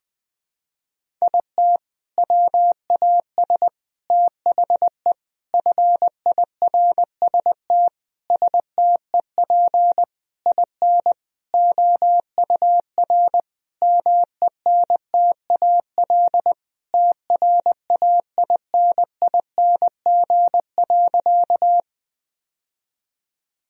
Morse code, heard as a keyed tone, about 21 wpm, 700 Hz. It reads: IT WAS THE FIRST STEP IN OUR MENTAL TRAINING.
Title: Sentences from Top 1000 Words - Review Part 4 - 20wpm